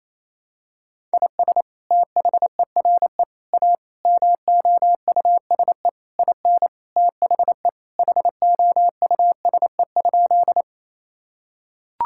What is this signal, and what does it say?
Morse code, heard as a keyed tone, about 28 wpm, 700 Hz. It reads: IS THERE A MOUSE IN THE HOUSE?
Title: Sentences from Top 1000 Words - Review Part 5 - 28wpm